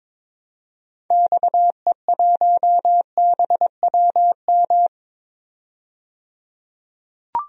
Morse code, heard as a keyed tone, about 22 wpm, 700 Hz. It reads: XE1BWM
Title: DXCC-100 Call Signs - 6 Characters - 22wpm